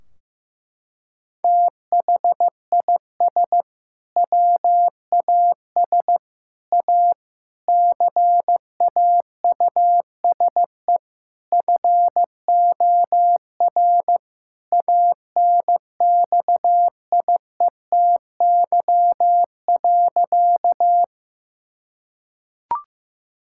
Morse code, heard as a keyed tone, about 15 wpm, 700 Hz. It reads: THIS WAS A CAUSE FOR ANXIETY.
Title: Sentences from Top 1600 Words 15wpm